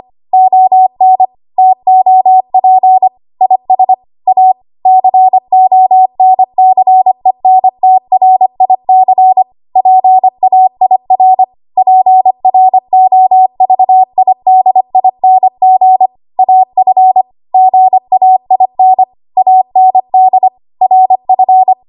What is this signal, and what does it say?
Morse code, heard as a keyed tone, about 25 wpm, 750 Hz. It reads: ON TOP IS A CONCENTRIC PAIR PROVIDING AF GAIN AND RF
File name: Archive / 25 WPM / 25WPM